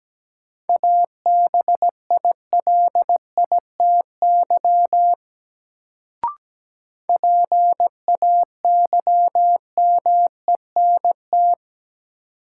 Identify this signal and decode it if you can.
Morse code, heard as a keyed tone, about 17 wpm, 700 Hz. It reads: ABILITY PAYMENT
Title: Sentences from Top 1200 Words - New Word Drill Alt - 17wpm